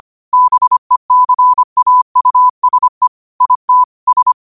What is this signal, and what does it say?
Morse code, heard as a keyed tone, about 25 wpm, 1000 Hz. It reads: BECAUSE IT S
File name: qotdfiles / 25wpm